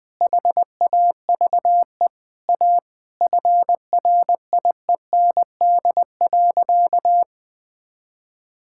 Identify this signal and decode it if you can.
Morse code, heard as a keyed tone, about 20 wpm, 700 Hz. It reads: HAVE A FRIEND.
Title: Sentences with 3-Word Phrases - 20wpm